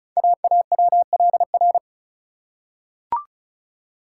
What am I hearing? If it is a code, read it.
Morse code, heard as a keyed tone, about 36 wpm, 700 Hz. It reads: AAWLR
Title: Code Groups - 5 Letters - 35wpm